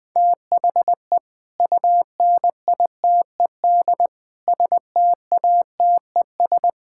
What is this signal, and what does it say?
Morse code, heard as a keyed tone, about 20 wpm, 700 Hz. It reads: THE UNITED STATES
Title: Sentences with 5-Word Phrases - Spoken Beginning - 20wpm